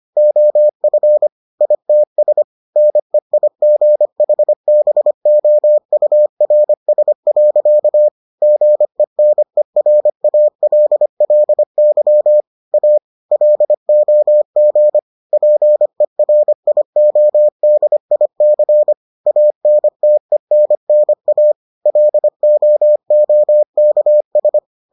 Morse code, heard as a keyed tone, about 25 wpm, 600 Hz. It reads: OF ITS NEIGHBOURS. GENERALLY A LOG PERIODIC ANTENNA LOOKS